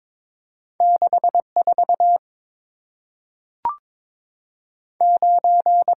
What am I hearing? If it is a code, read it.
Morse code, heard as a keyed tone, about 22 wpm, 700 Hz. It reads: 6 4 9
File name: practice / 22wpm / Two Letter-Number Combinations - 22wpm